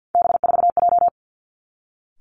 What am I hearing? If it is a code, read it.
Morse code, heard as a keyed tone, about 50 wpm, 700 Hz. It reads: B4J
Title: Three Challenging Letter-Number Combinations 50wpm